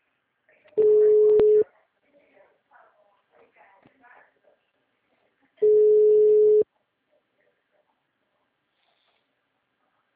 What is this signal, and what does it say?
Ringback tone (Europe, 425 Hz), with speech.